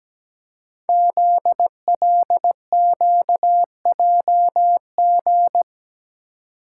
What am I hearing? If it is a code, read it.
Morse code, heard as a keyed tone, about 17 wpm, 700 Hz. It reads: ZLQJG
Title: Code Groups - 5 Letters - 17wpm